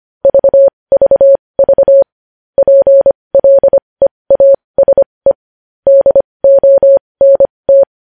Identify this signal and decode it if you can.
Morse code, heard as a keyed tone, about 25 wpm, spaced as for 20 wpm, 550 Hz. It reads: VVV PLEASE DONT
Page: news-headlines-in-morse-code-at-20-wpm